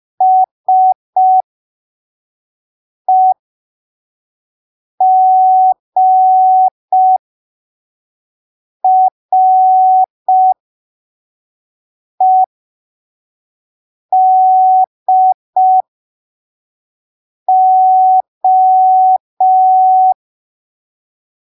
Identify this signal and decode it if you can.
Morse code, heard as a keyed tone, about 5 wpm, spaced as for 3 wpm, 750 Hz. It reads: SEGREDO
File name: arquivos / texto / texto 20 05ppm